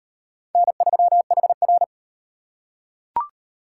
Morse code, heard as a keyed tone, about 35 wpm, 700 Hz. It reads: N3HR